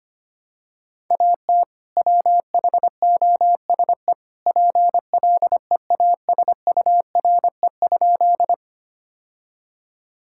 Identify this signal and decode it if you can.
Morse code, heard as a keyed tone, about 25 wpm, 700 Hz. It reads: AT WHOSE PLEASURE?
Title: Sentences from Top 1600 Words 25wpm